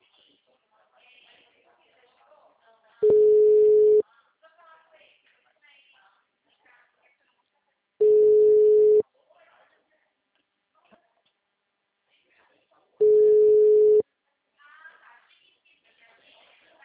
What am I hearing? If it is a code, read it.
Ringback tone (Europe, 425 Hz), with speech.
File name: out- -2002- - - .18451